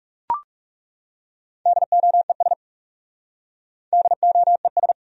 Morse code, heard as a keyed tone, about 37 wpm, 700 Hz. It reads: DOES DOES
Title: Speed-Racing Top 500 Words 30wpm